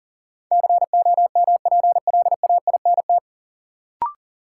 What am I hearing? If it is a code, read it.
Morse code, heard as a keyed tone, about 38 wpm, 700 Hz. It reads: COMPLAINT
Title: Sentences from Top 1400 Words - New Word Drill Alt - 40wpm